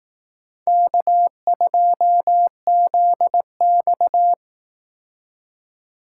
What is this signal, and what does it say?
Morse code, heard as a keyed tone, about 18 wpm, 700 Hz. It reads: K2ZX
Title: Speed Racing - USA 1x2 Call Signs 15wpm